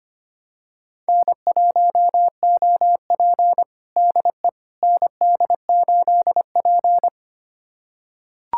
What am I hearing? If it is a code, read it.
Morse code, heard as a keyed tone, about 25 wpm, 700 Hz. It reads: N1OP DE ND8P
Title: USA Callsign DE Callsign - 4 Characters - 25wpm